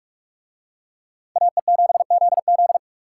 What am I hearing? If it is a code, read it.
Morse code, heard as a keyed tone, about 43 wpm, 700 Hz. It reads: AE7ZZ